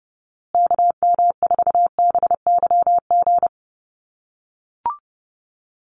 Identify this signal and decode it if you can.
Morse code, heard as a keyed tone, about 30 wpm, 700 Hz. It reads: KM4BYG
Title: USA Call Signs Repeated 3x 30wpm